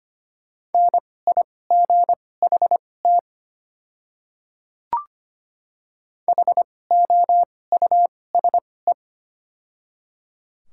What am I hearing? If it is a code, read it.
Morse code, heard as a keyed tone, about 25 wpm, spaced as for 18 wpm, 690 Hz. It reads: NIGHT HOUSE
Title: 2x Character Spacing - Top 300 Words - 25wpm